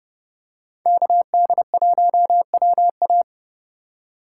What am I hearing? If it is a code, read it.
Morse code, heard as a keyed tone, about 30 wpm, 700 Hz. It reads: KD1WA
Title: USA Call Signs - 5 Characters - 4x Character Spacing - 30wpm